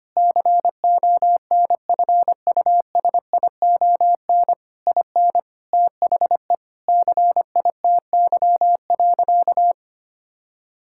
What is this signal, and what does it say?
Morse code, heard as a keyed tone, about 25 wpm, 700 Hz. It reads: CONFUSION IN THE CITY.
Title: Sentences from Top 1600 Words 25wpm